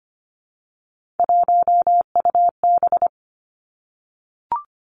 Morse code, heard as a keyed tone, about 25 wpm, 700 Hz. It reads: 1UB
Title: Three Challenging Letter-Number Combinations 25wpm